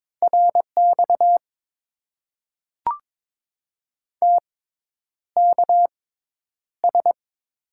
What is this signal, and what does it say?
Morse code, heard as a keyed tone, about 22 wpm, spaced as for 10 wpm, 700 Hz. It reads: RX TKS
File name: Top 90 QSO Elements - 6x Character Spacing - 22wpm